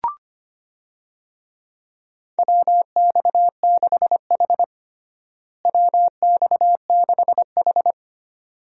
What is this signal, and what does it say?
Morse code, heard as a keyed tone, about 25 wpm, 700 Hz. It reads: WX6H WX6H
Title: USA Call Signs - 4 Characters - Repeated 2x - 25wpm